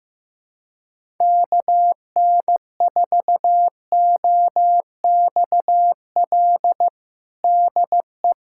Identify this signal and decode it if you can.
Morse code, heard as a keyed tone, about 15 wpm, 700 Hz. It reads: KN4OXL DE
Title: USA Callsign DE Callsign 15wpm